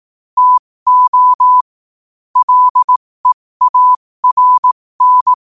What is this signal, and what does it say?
Morse code, heard as a keyed tone, about 18 wpm, spaced as for 15 wpm, 1000 Hz. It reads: TO LEARN
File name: qotdfiles / 15wpm